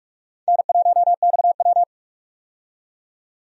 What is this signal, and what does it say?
Morse code, heard as a keyed tone, about 46 wpm, 700 Hz. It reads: N1XW
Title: USA Call Signs - 4 Characters - 45wpm